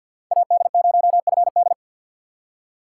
Morse code, heard as a keyed tone, about 53 wpm, 700 Hz. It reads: AD0FD
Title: USA Call Signs - 5 Characters - 8x Character Spacing - 50wpm